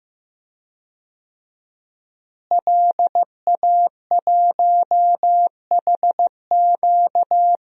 Morse code, heard as a keyed tone, about 15 wpm, 700 Hz. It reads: LA1HQ
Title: DXCC-100 Call Signs - 5 Characters - Repeated 3x - 15wpm